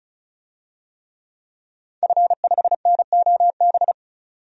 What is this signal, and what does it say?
Morse code, heard as a keyed tone, about 36 wpm, 700 Hz. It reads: F5NOB
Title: DXCC-100 Call Signs - 5 Characters - 35wpm